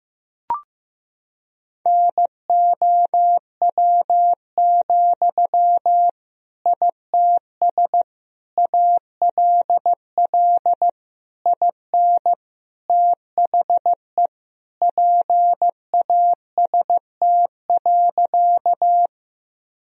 Morse code, heard as a keyed tone, about 16 wpm, 700 Hz. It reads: NOW, ITS ALL IN THE PAST.
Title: Sentences from Top 1000 Words - Review Part 3 - 15wpm